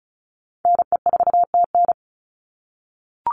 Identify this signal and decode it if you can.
Morse code, heard as a keyed tone, about 34 wpm, 700 Hz. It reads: NE4TN
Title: NA SOTA Chaser Callsigns 35wpm